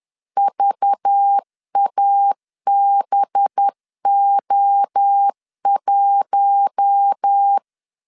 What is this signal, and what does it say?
Morse code, heard as a keyed tone, about 10 wpm, 800 Hz. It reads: VABO1